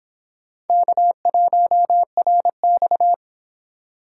Morse code, heard as a keyed tone, about 26 wpm, 700 Hz. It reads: K1RX